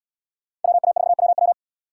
Morse code, heard as a keyed tone, about 69 wpm, 700 Hz. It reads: WA4WW